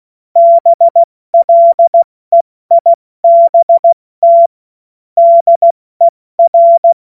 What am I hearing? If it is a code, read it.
Morse code, heard as a keyed tone, about 16 wpm, spaced as for 14 wpm, 680 Hz. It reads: BLEIBT DER
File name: Morsekurs / Witze / Witz 30. Dezember 2025 70 F80